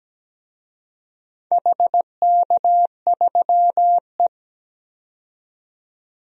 Morse code, heard as a keyed tone, about 17 wpm, 700 Hz. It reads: HK3E
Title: DXCC-100 Call Signs - 4 Characters - 17wpm